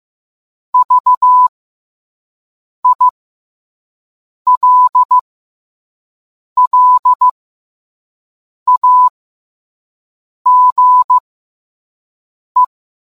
Morse code, heard as a keyed tone, about 15 wpm, spaced as for 5 wpm, 1000 Hz. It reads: VILLAGE